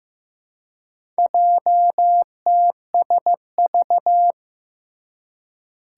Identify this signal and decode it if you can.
Morse code, heard as a keyed tone, about 15 wpm, 700 Hz. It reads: JTSV
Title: Four Challenging Letter-Number Combinations 15wpm